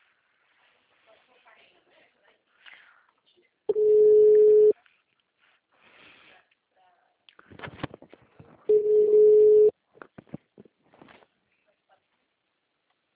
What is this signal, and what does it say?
Ringback tone (Europe, 425 Hz), with speech.
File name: out- -2042- - - .18403